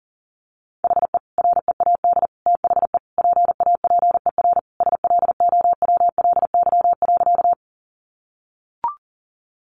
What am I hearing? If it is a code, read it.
Morse code, heard as a keyed tone, about 40 wpm, 700 Hz. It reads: HE READ THE PAPER SLOWLY.
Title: Sentences from Top 1100 Words 40wpm